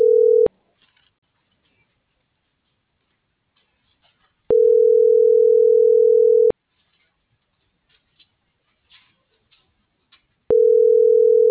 Ringback tone (North America), 440 + 480 Hz.